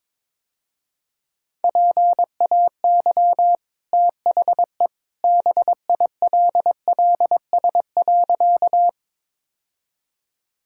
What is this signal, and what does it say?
Morse code, heard as a keyed tone, about 22 wpm, 700 Hz. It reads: PAY THE BILLS.
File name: practice / 22wpm / Easy Sentences with Hints - 22wpm